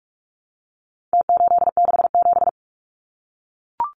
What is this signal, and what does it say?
Morse code, heard as a keyed tone, about 44 wpm, 700 Hz. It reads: T867
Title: Numbers - Contest Style - 4-Digit 45wpm